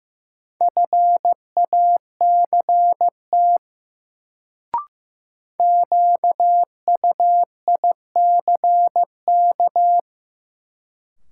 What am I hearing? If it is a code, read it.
Morse code, heard as a keyed tone, about 15 wpm, 700 Hz. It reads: FACT QUICK
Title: Sentences from Top 500 Words - New Word Drill - 15wpm